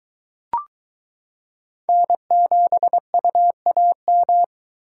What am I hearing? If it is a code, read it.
Morse code, heard as a keyed tone, about 22 wpm, 700 Hz. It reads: N7UAM